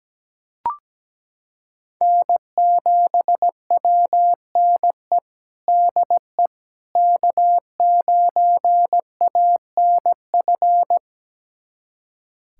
Morse code, heard as a keyed tone, about 17 wpm, 700 Hz. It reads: N7WNE DE K9ANF
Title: USA Callsign DE Callsign - 5 Characters - 17wpm